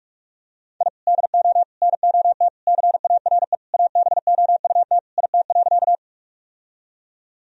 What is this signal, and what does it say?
Morse code, heard as a keyed tone, about 41 wpm, 700 Hz. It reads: I DO NOT CARE ABOUT IT.
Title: Sentences from Top 1000 Words - Review Part 1 - 45wpm